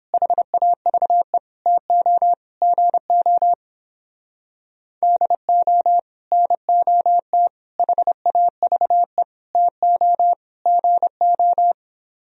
Morse code, heard as a keyed tone, about 29 wpm, 700 Hz. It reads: HAVE TO GO DO NOT HAVE TO GO